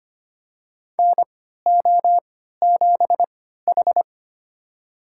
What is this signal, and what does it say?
Morse code, heard as a keyed tone, about 25 wpm, spaced as for 14 wpm, 700 Hz. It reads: NO7H
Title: USA Call Signs - 4 Characters - 3x Character Spacing - 25wpm